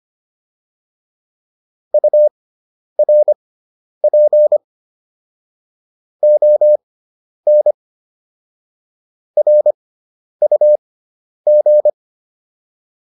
Morse code, heard as a keyed tone, about 25 wpm, spaced as for 10 wpm, 600 Hz. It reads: URP ON RUG